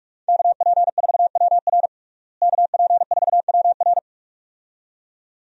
Morse code, heard as a keyed tone, about 45 wpm, 700 Hz. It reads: KP4WR KP4WR
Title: USA Call Signs - 5 Characters - Repeated 2x - 45wpm